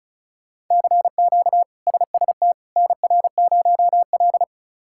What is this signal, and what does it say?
Morse code, heard as a keyed tone, about 34 wpm, 700 Hz. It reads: CQ SST NR0L
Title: K1USN SST QSO Practice - 35wpm